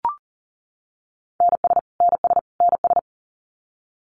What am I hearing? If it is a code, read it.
Morse code, heard as a keyed tone, about 40 wpm, 700 Hz. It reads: NS NS NS
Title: Canadian Provinces and Territories - Repeated 3x - 40wpm